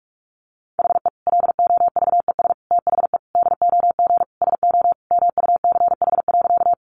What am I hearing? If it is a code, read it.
Morse code, heard as a keyed tone, about 44 wpm, 700 Hz. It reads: HE LOVES THE DOG SO MUCH.
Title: Additional Easy Phrases 45wpm